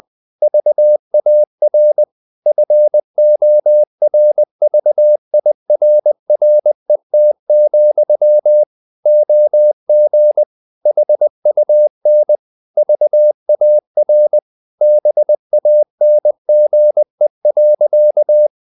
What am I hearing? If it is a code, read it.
Morse code, heard as a keyed tone, about 20 wpm, 600 Hz. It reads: VAR FORVIRRET, OG HUN VAR BANGE.
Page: Ferie 20 wpm